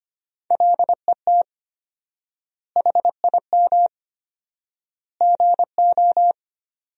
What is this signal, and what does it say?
Morse code, heard as a keyed tone, about 25 wpm, 700 Hz. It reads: LET HIM GO
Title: 4x Word-Spacing - Sentences from Top 500 Words - 25wpm